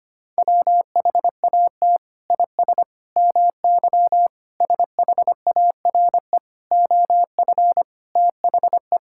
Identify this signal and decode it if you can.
Morse code, heard as a keyed tone, about 25 wpm, 700 Hz. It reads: WHAT IS MY SHARE OF THE
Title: Sentences from Top 1000 Words - Review Part 6 - 25wpm